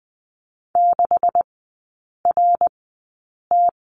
Morse code, heard as a keyed tone, about 20 wpm, spaced as for 8 wpm, 700 Hz. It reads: 6RT
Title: 172 Lesson 28e - Number 9 - Three Characters 20wpm